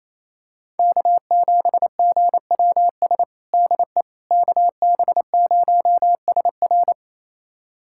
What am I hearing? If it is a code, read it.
Morse code, heard as a keyed tone, about 28 wpm, 700 Hz. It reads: K7GWS DE KB0SR
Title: USA Callsign DE Callsign - 5 Characters - 28wpm